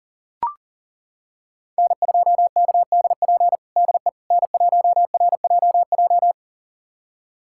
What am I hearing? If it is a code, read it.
Morse code, heard as a keyed tone, about 40 wpm, 700 Hz. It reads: N2KDP DE N1RJJ
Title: USA Callsign DE Callsign - 5 Characters - 40wpm